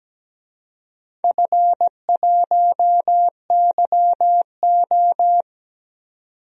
Morse code, heard as a keyed tone, about 17 wpm, 700 Hz. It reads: F1YO
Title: DXCC-100 Call Signs - 4 Characters - 17wpm